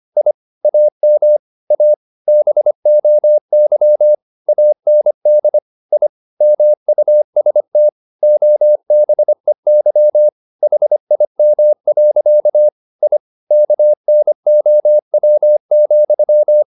Morse code, heard as a keyed tone, about 25 wpm, 600 Hz. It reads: I AM A BOY AND I MUST OBEY HIM. I KNOW,
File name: the old man and the sea part 1 25wpm